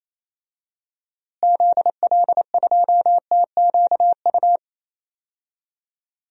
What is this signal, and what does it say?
Morse code, heard as a keyed tone, about 28 wpm, 700 Hz. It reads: ZL2TQU
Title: DXCC-100 Call Signs - 6 Characters - Repeated 2x - 28wpm